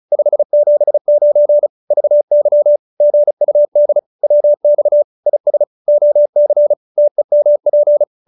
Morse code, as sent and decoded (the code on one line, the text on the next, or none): ..... --... ----. / ...- -.-- / --. ..- -.. / .-- -..- / .. ... / --- -.-. / - . -- .--.
579 VY GUD WX IS OC TEMP